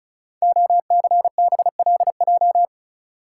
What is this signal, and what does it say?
Morse code, heard as a keyed tone, about 36 wpm, 690 Hz. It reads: OCBLJ
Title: Code Groups - 5 Letters - 35wpm